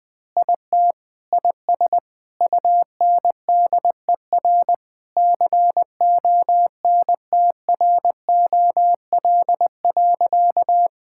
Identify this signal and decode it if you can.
Morse code, heard as a keyed tone, about 21 wpm, 700 Hz. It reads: IT IS UNDER CONTROL.
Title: Sentences with 9-Word Phrases - 20wpm